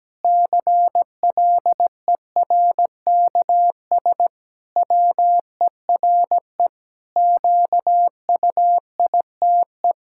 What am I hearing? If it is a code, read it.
Morse code, heard as a keyed tone, about 17 wpm, 700 Hz. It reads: CLERKS WERE QUITE
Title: Pangram Challenge v2 - 17wpm